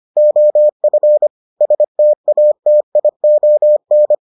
Morse code, heard as a keyed tone, about 25 wpm, 600 Hz. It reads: OF STATION